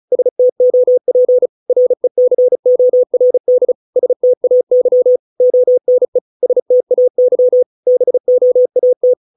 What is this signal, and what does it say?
Morse code, heard as a keyed tone, about 34 wpm, 500 Hz. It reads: STOP RECORD STAY ONE STAY BOAT
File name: 632 words / 632 Words 35WPM